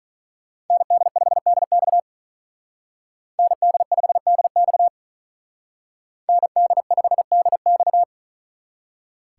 Morse code, heard as a keyed tone, about 41 wpm, 700 Hz. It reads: ND5DX ND5DX ND5DX
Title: Speed Racing - USA 2x2 Call Signs 35wpm